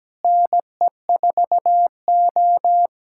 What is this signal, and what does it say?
Morse code, heard as a keyed tone, about 17 wpm, 700 Hz. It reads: NE4O
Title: USA Callsign DE Callsign - 4 Characters - 17wpm